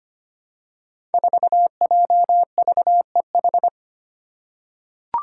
Morse code, heard as a keyed tone, about 25 wpm, 700 Hz. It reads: 4JVEH
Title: Five Challenging Letter-Number Combinations 25wpm